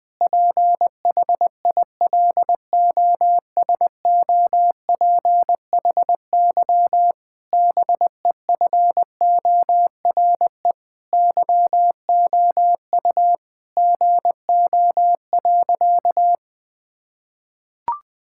Morse code, heard as a keyed tone, about 20 wpm, 700 Hz. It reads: PHILOSOPHY BEFORE YOU GO.
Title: Sentences from Top 1000 Words - Review Part 6 - 20wpm